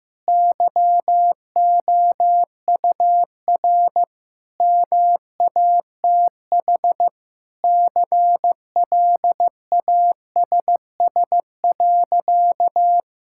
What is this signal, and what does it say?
Morse code, heard as a keyed tone, about 16 wpm, 700 Hz. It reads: YOUR MATH CLASS.
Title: Sentences from Top 1000 Words - Review Part 6 - 15wpm